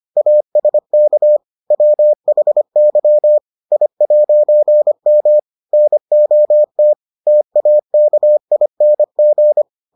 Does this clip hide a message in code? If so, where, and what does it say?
Yes: Morse code from 0.2 to 9.6 s, heard as a keyed tone, about 25 wpm, 600 Hz. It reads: ASK WHY I'M NOT TAKING